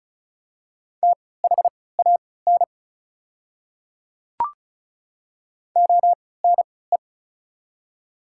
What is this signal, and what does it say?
Morse code, heard as a keyed tone, about 36 wpm, spaced as for 20 wpm, 700 Hz. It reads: THAN ONE
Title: 3x Character Spacing - Top 100 Words - 35wpm